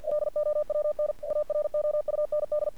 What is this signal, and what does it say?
Morse code, heard as a keyed tone, about 46 wpm, 600 Hz. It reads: LOWN AROUND
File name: archived / CRSnet